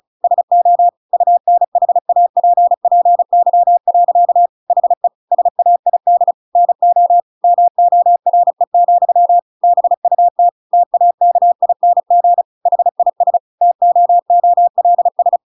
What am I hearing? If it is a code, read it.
Morse code, heard as a keyed tone, about 36 wpm, 700 Hz. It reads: SO UNHAPPY. HE SAID NO MORE, BUT TAKING HIS TOOLS
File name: Pinocchio / Pinocchio ch8 35 35 ik1hge